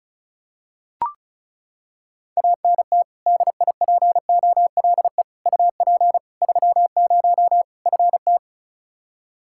Morse code, heard as a keyed tone, about 34 wpm, 700 Hz. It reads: ANT DIPOLE UP 30 FT
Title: K1USN SST QSO Practice 35wpm